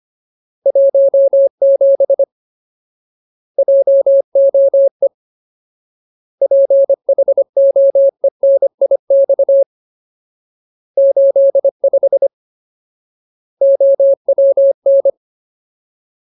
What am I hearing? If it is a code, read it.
Morse code, heard as a keyed tone, about 25 wpm, 550 Hz. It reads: 17 JOE PHOENIX 85 OWN
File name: Ultimate Word List / 25WPM Ultimate